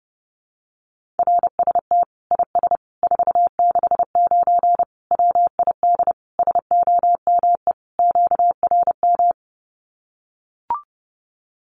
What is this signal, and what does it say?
Morse code, heard as a keyed tone, about 30 wpm, 700 Hz. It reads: RST IS 469 WID SOME QRM
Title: Common QSO Phrases 30wpm